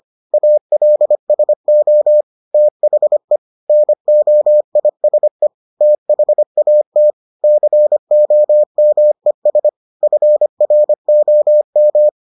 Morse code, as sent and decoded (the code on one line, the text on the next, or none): .- .-.. ... --- / - .... / -. --- .. ... / - .... .- - / -.-. --- -- . ... / ..-. .-. --- --
ALSO THE NOISE THAT COMES FROM